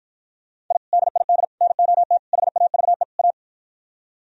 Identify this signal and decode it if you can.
Morse code, heard as a keyed tone, about 52 wpm, 700 Hz. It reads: I DID NOT HAVE A